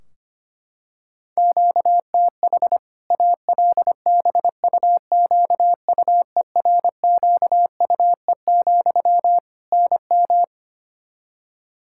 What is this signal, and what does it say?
Morse code, heard as a keyed tone, about 25 wpm, 700 Hz. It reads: QTH ALBUQUERQUE, NM
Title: Practice QSO - Exchange 1 - Beginning - 25wpm